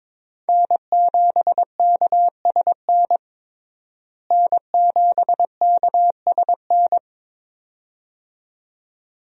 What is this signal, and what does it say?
Morse code, heard as a keyed tone, about 21 wpm, 700 Hz. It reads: N7KSN N7KSN
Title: USA Call Signs - 5 Characters - Repeated 2x - 22wpm